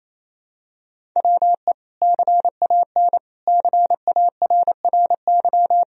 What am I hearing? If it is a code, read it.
Morse code, heard as a keyed tone, about 28 wpm, 700 Hz. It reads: WE CAN CARRY
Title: Sentences from Top 1000 Words - Review Part 1 - 28wpm